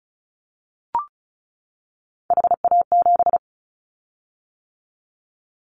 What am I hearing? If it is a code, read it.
Morse code, heard as a keyed tone, about 34 wpm, 700 Hz. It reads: HA7
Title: Callsign Prefixes for Top 25 DXCC - 35wpm